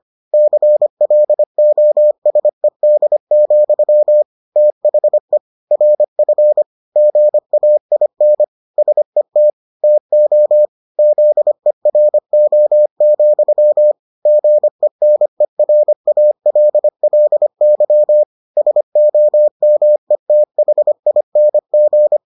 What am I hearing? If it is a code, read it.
Morse code, heard as a keyed tone, about 25 wpm, 600 Hz. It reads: CLOSED, THE RF GAIN SET TO ZERO, GENERALLY SOMETHING